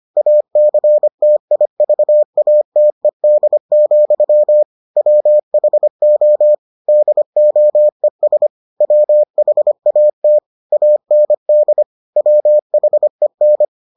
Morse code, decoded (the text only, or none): ACTIVATED, WHO DOES WHAT AND WHEN